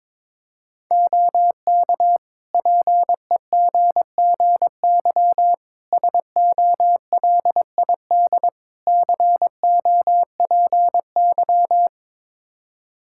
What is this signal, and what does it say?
Morse code, heard as a keyed tone, about 22 wpm, 700 Hz. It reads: OK PEGGY SOLID COPY
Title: Practice QSO - Exchange 3 - More Info - 22wpm